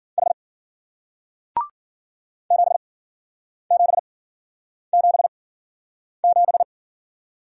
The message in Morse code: H 7777

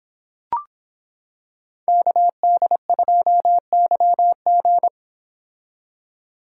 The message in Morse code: KD2YG

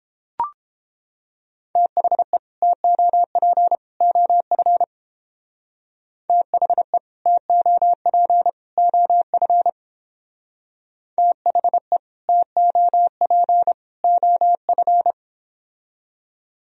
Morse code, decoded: THE TOP OF THE TOP OF THE TOP OF

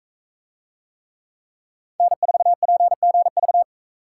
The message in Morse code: N4PGV

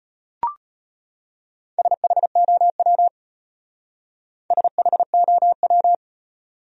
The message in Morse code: SHOW SHOW

